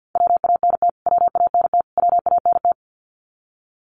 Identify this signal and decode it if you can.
Morse code, heard as a keyed tone, about 50 wpm, 700 Hz. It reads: WANT WANT WANT
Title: Top 300 Words Repeated 50wpm